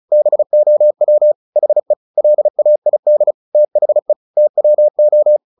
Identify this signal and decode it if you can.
Morse code, heard as a keyed tone, about 34 wpm, 600 Hz. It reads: BOW HE LAID THE TWO
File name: the old man and the sea 2 35wpm